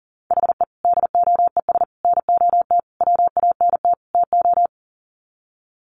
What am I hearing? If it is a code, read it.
Morse code, heard as a keyed tone, about 40 wpm, 700 Hz. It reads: HE DOES NOT WANT TO
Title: Sets of 5 Words 40wpm